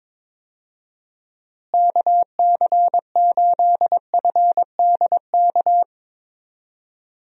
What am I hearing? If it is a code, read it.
Morse code, heard as a keyed tone, about 22 wpm, 700 Hz. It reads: KC8FDK
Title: USA Call Signs - 6 Characters - Repeated 3x - 22wpm